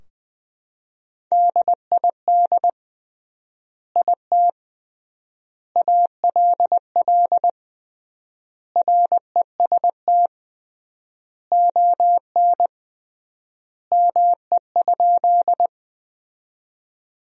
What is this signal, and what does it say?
Morse code, heard as a keyed tone, about 21 wpm, 700 Hz. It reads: DID IT ALL REST ON ME?
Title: 3x Word-Spacing - Sentences from Top 500 Words - 20wpm